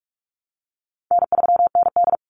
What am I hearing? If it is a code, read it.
Morse code, heard as a keyed tone, about 44 wpm, 700 Hz. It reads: N3ND
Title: Easy Call Signs Repeated 3x - 45wpm